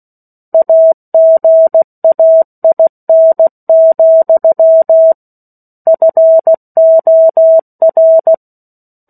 Morse code, heard as a keyed tone, about 16 wpm, 650 Hz. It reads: AGAIN, FOR
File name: 03-Chapter-03 16 WPM